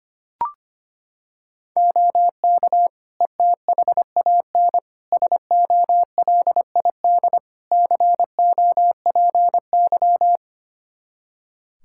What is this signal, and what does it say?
Morse code, heard as a keyed tone, about 25 wpm, 700 Hz. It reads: OK ETHAN SOLID COPY
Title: Practice QSO - Exchange 3 - More Info - 25wpm